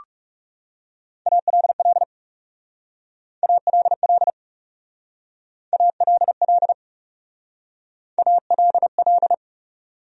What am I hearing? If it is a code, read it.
Morse code, heard as a keyed tone, about 38 wpm, 700 Hz. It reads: ALL ALL ALL ALL